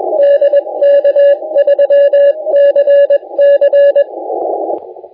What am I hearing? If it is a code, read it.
Morse code, heard as a keyed tone, about 21 wpm, 570 Hz, with radio noise. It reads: DK3CC
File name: la4c / sac-cw-2018 / CONTEST AUDIO 2200Z